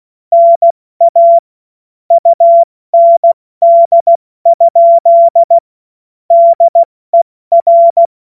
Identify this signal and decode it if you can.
Morse code, heard as a keyed tone, about 16 wpm, 680 Hz. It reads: NA UND? DER
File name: Morsekurs / Witze / Witz 18. Mai 2021, 70 F80